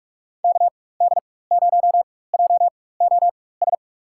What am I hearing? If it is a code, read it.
Morse code, heard as a keyed tone, about 43 wpm, spaced as for 21 wpm, 700 Hz. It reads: KD0JOS
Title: USA Call Signs - 4x Character Spacing - 45wpm